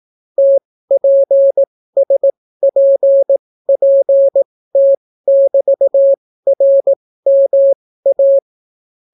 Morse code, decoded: TPSPPT<BT>RMA

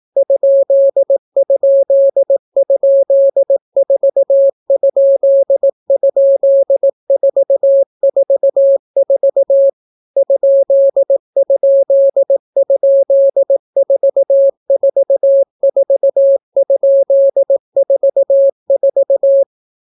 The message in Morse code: ???4??444 ???444?44

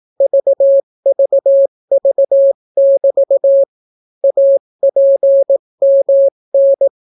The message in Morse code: VVV<BT> APMN

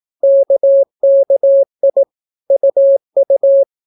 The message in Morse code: KKI UU

